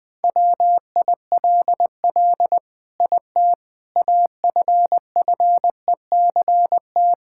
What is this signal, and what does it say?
Morse code, heard as a keyed tone, about 21 wpm, 700 Hz. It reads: WILL IT AFFECT
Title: Sentences from Top 1000 Words - Review Part 6 - 20wpm